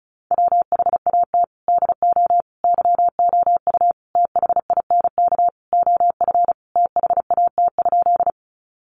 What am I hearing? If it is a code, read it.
Morse code, heard as a keyed tone, about 34 wpm, 700 Hz. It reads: WHAT DO YOU THINK OF THAT?
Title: Sentences from Top 200 Words 35wpm